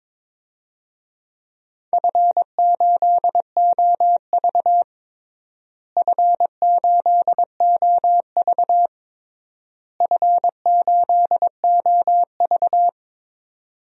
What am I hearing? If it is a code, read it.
Morse code, heard as a keyed tone, about 22 wpm, 700 Hz. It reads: F8OV F8OV F8OV
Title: DXCC-100 Call Signs - 4 Characters - Repeated 3x - 22wpm